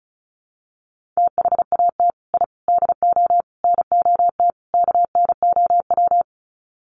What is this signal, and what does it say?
Morse code, heard as a keyed tone, about 34 wpm, 700 Hz. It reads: THAT I DO NOT KNOW
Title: Sets of 5 Words Encore2 35wpm